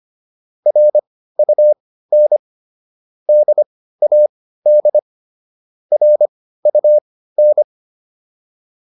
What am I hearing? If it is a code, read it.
Morse code, heard as a keyed tone, about 25 wpm, spaced as for 15 wpm, 600 Hz. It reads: RUN DAD RUN